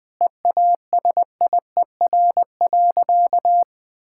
Morse code, heard as a keyed tone, about 21 wpm, 700 Hz. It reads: EASIER.
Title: Difficult Sentences with Hints - 20wpm